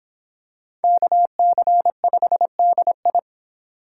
Morse code, heard as a keyed tone, about 24 wpm, 700 Hz. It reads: KC5DI